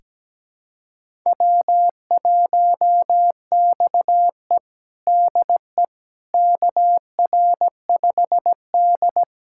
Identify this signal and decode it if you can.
Morse code, heard as a keyed tone, about 17 wpm, 700 Hz. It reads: W1XE DE KR5D
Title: USA Callsign DE Callsign - 4 Characters - 17wpm